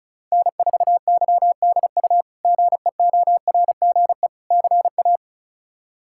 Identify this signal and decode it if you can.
Morse code, heard as a keyed tone, about 34 wpm, 700 Hz. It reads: N4YDU GEORGE CA
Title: K1USN SST QSO Practice - 35wpm